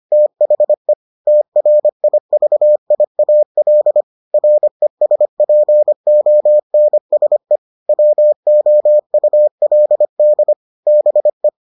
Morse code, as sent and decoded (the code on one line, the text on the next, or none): - .... / - .-. .. ...- .. .- .-.. / .-. ... .--. --- -. ... / .-- --- ..- .-.. -.. / -... .
THE TRIVIAL RESPONSE WOULD BE